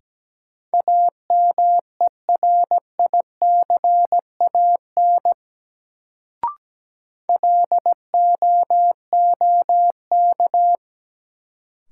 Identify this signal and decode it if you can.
Morse code, heard as a keyed tone, about 17 wpm, 700 Hz. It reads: AMERICAN LOOK
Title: Sets of 2 Words - New Word Drill - 17wpm